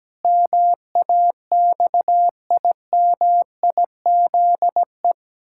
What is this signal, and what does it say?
Morse code, heard as a keyed tone, about 17 wpm, 700 Hz. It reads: MAXIMIZE
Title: Pangram Challenge v2 - 17wpm